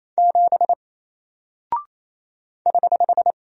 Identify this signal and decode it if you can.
Morse code, heard as a keyed tone, about 28 wpm, 700 Hz. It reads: 7 <HH>
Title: Single Character (Letter:Number:Punctuation:Prosign) Alt 28wpm